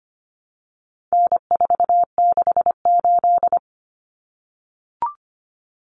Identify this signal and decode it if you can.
Morse code, heard as a keyed tone, about 25 wpm, 700 Hz. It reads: N468
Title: Common Cut Numbers - 4-Digit 25wpm